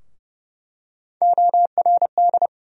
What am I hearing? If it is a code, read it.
Morse code, heard as a keyed tone, about 30 wpm, 700 Hz. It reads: ORD
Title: USA Call Sign Postfixes - 3 Letters 30wpm